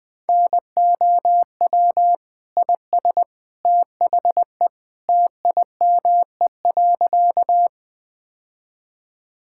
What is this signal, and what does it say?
Morse code, heard as a keyed tone, about 19 wpm, 700 Hz. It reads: NOW IS THE TIME.
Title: Sentences with 2-Word Phrases - Spoken Beginning - 20wpm